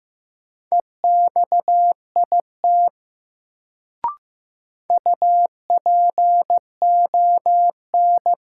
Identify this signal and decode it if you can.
Morse code, heard as a keyed tone, about 15 wpm, 700 Hz. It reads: EXIT UPON
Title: Sentences from Top 1100 Words - New Word Drill Alt - 15wpm